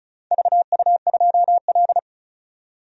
Morse code, heard as a keyed tone, about 34 wpm, 700 Hz. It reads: VU2L
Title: DXCC-100 Call Signs - 4 Characters - 35wpm